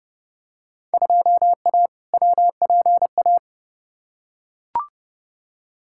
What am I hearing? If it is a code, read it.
Morse code, heard as a keyed tone, about 30 wpm, 700 Hz. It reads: 2A WPA
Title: Field Day Exchanges - 30wpm